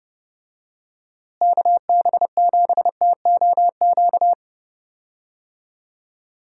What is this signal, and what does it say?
Morse code, heard as a keyed tone, about 31 wpm, 700 Hz. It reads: KB7TOQ
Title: USA Call Signs - 6 Characters - Repeated 2x - 30wpm